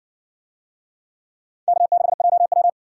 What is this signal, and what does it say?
Morse code, heard as a keyed tone, about 55 wpm, 700 Hz. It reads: K6JW